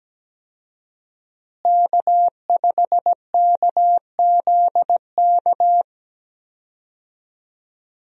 Morse code, heard as a keyed tone, about 17 wpm, 700 Hz. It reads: K5KZK
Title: USA Call Signs - 5 Characters - Repeated 3x - 17wpm